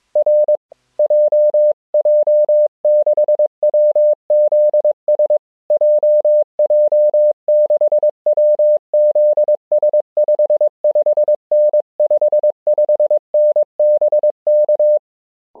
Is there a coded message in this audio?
Yes: Morse code from 0.2 to 15.0 s, heard as a keyed tone, about 22 wpm, 600 Hz. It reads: R JJ6WZS JJ6WZS 55N 55N BK